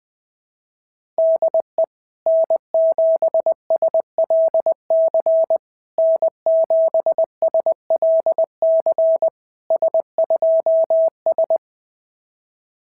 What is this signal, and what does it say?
Morse code, heard as a keyed tone, about 19 wpm, 650 Hz. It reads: DE N7SLC N7SLC S2S
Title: SOTA QSO Practice 20wpm